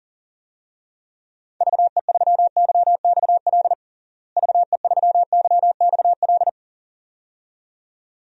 Morse code, heard as a keyed tone, about 40 wpm, 700 Hz. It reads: VE3YXL VE3YXL
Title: DXCC-100 Call Signs - 6 Characters - Repeated 2x - 40wpm